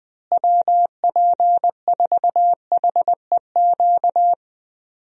Morse code, heard as a keyed tone, about 20 wpm, 700 Hz. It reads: WP4HEQ
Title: USA Call Signs - 6 Characters - Repeated 2x - 20wpm